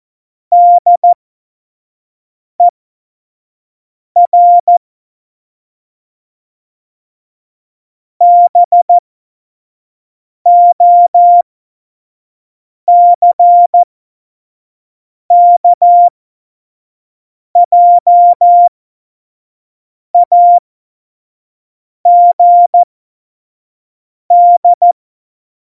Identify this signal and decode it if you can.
Morse code, heard as a keyed tone, about 14 wpm, spaced as for 5 wpm, 700 Hz. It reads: DER BOCKJAGD
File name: Morsekurs / Witze / Witz 27 Juni 17, Tempo 25 F70